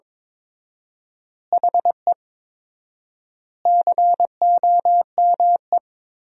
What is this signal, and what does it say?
Morse code, heard as a keyed tone, about 22 wpm, 700 Hz. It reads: HE COME
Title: 4x Word-Spacing - Sentences from Top 200 Words - 22wpm